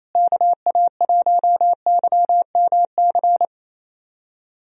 Morse code, heard as a keyed tone, about 27 wpm, 700 Hz. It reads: KA1YMC
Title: USA Call Signs - 6 Characters - Repeated 3x - 28wpm